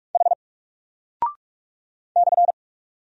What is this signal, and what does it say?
Morse code, heard as a keyed tone, about 45 wpm, 700 Hz. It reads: H /
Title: Single Character (Letter:Number:Punctuation:Prosign) Alt 45wpm